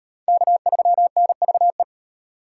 Morse code, heard as a keyed tone, about 34 wpm, 700 Hz. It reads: K3NVE